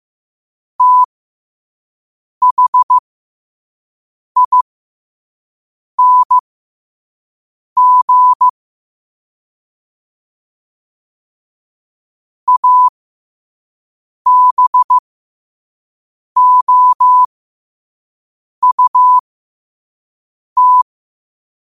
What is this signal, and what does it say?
Morse code, heard as a keyed tone, about 15 wpm, spaced as for 5 wpm, 1000 Hz. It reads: THING ABOUT
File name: qotdfiles / 5wpm